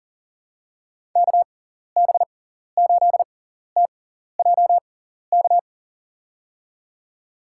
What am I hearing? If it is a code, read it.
Morse code, heard as a keyed tone, about 37 wpm, spaced as for 14 wpm, 700 Hz. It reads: KB8TJK
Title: USA Call Signs - 6 Characters - 6x Character Spacing - 40wpm